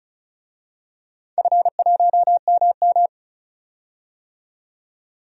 Morse code, heard as a keyed tone, about 34 wpm, 700 Hz. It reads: F1MM
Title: DXCC-100 Call Signs - 4 Characters - 35wpm